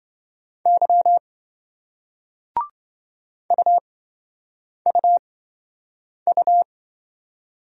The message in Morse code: Y UUU

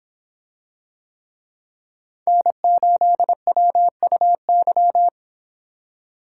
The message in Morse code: N8WUY

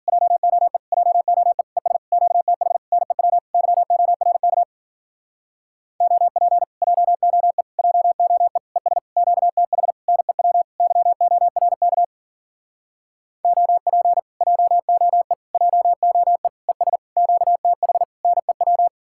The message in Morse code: JOE JOE ES QTH NEW YORK OP JOE JOE ES QTH NEW YORK OP JOE JOE ES QTH NEW

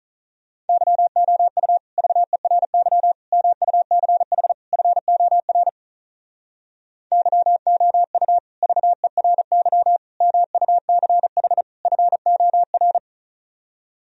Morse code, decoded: YOU VERY MUCH FOR YOU VERY MUCH FOR

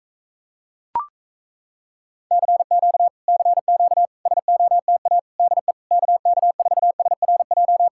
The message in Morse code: CQ CQ SOTA DE KK4SRJ

